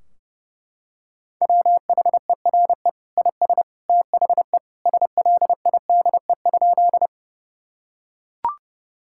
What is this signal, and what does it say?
Morse code, heard as a keyed tone, about 32 wpm, 700 Hz. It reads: WHERE IS THE SLIDE?
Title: Sentences from Top 1600 Words 30wpm